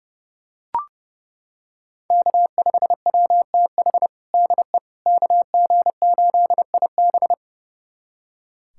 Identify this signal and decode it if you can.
Morse code, heard as a keyed tone, about 32 wpm, 700 Hz. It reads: K5WTH DE KG8IB
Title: USA Callsign DE Callsign - 5 Characters - 30wpm